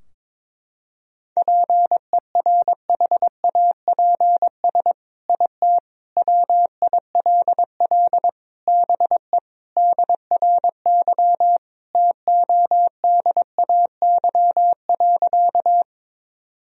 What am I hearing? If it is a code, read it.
Morse code, heard as a keyed tone, about 22 wpm, 700 Hz. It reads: PERHAPS IT WILL BE DRY TODAY.
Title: Sentences from Top 1000 Words - Review Part 3 - 22wpm